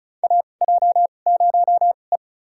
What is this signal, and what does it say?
Morse code, heard as a keyed tone, about 34 wpm, spaced as for 25 wpm, 700 Hz. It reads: AJ0E
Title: USA Call Signs - 4 Characters - 2x Character Spacing - 35wpm